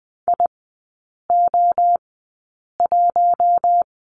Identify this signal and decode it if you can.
Morse code, heard as a keyed tone, about 20 wpm, spaced as for 8 wpm, 700 Hz. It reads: IO1
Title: 041 Lesson 8e - Number 1 - Three Characters 20wpm